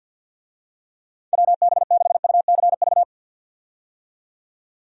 Morse code, heard as a keyed tone, about 52 wpm, 700 Hz. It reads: WB6UCV
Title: USA Call Signs - 6 Characters - 50wpm